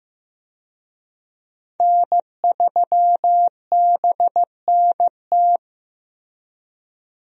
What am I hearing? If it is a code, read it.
Morse code, heard as a keyed tone, about 15 wpm, 700 Hz. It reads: N3BNT
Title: USA Call Signs - 5 Characters - Repeated 2x - 15wpm